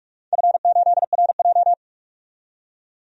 Morse code, heard as a keyed tone, about 46 wpm, 700 Hz. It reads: F8RJ
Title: DXCC-100 Call Signs - 4 Characters - 45wpm